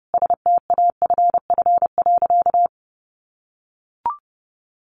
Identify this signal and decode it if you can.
Morse code, heard as a keyed tone, about 30 wpm, 700 Hz. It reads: STAFF.
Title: Sentences from Top 1200 Words 30wpm